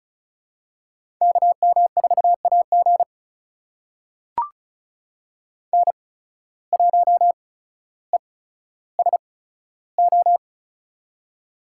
Morse code, heard as a keyed tone, about 36 wpm, spaced as for 15 wpm, 700 Hz. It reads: KM4AG N1ESO
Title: USA Call Signs - 8x Character Spacing - 35wpm